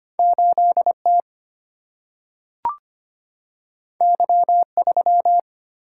Morse code, heard as a keyed tone, about 25 wpm, 700 Hz. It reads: 8 T Y 3
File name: Two Letter-Number Combinations - 25wpm